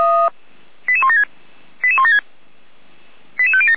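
Five-tone selective call (ZVEI), heard as quick runs of five tones, tones of about 70 ms. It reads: 88176 88176 88507
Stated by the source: ZVEI five-tone selective call 88191 received via SDR.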